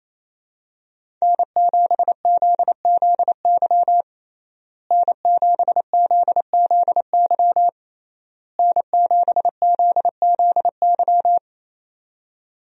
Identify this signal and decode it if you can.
Morse code, heard as a keyed tone, about 27 wpm, 700 Hz. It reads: N7ZZY N7ZZY N7ZZY
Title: USA Call Signs - 5 Characters - Repeated 3x - 28wpm